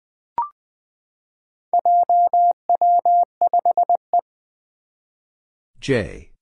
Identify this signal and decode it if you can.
Morse code, heard as a keyed tone, about 20 wpm, 700 Hz. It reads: JW5E
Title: Four Challenging Letter-Number Combinations 20wpm